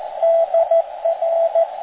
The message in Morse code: DR